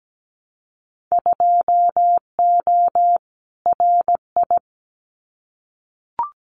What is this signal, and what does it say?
Morse code, heard as a keyed tone, about 17 wpm, 700 Hz. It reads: 2O RI
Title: Winter Field Day Exchanges 17wpm